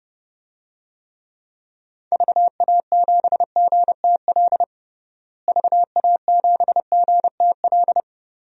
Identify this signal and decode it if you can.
Morse code, heard as a keyed tone, about 32 wpm, 700 Hz. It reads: VA7GTL VA7GTL
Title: DXCC-100 Call Signs - 6 Characters - Repeated 3x - 30wpm